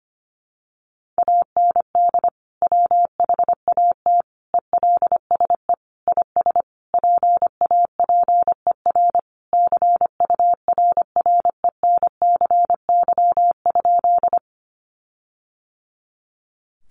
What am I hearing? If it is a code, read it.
Morse code, heard as a keyed tone, about 25 wpm, 700 Hz. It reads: AND WHAT ELSE IS PAPER CURRENCY?
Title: Sentences from Top 1500 Words 25wpm